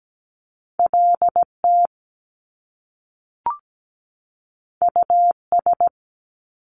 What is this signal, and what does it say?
Morse code, heard as a keyed tone, about 17 wpm, 700 Hz. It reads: LT US
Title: Top Two Letter Combinations in Words 17wpm